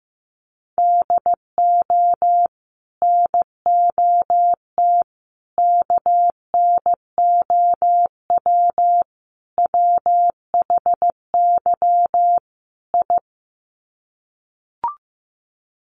Morse code, heard as a keyed tone, about 15 wpm, 700 Hz. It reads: DO NOT KNOW WHY I
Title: Sets of 5 Words Encore 15wpm